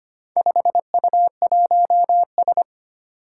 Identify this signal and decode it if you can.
Morse code, heard as a keyed tone, about 25 wpm, 700 Hz. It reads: 5U1S
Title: Four Challenging Letter-Number Combinations 25wpm